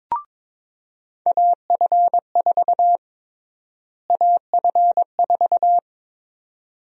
Morse code, heard as a keyed tone, about 22 wpm, 700 Hz. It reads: AF4 AF4
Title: USA Call Sign Prefixes - Repeated 2x - 22wpm